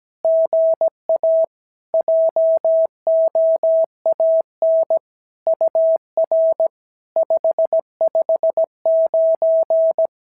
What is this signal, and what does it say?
Morse code, heard as a keyed tone, about 17 wpm, 650 Hz. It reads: GA JOAN UR 559
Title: SOTA QSO Practice 17wpm